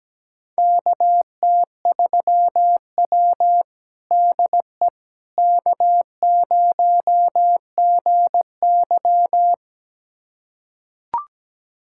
Morse code, heard as a keyed tone, about 17 wpm, 700 Hz. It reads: KT3W DE K0GY
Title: USA Callsign DE Callsign - 4 Characters - 17wpm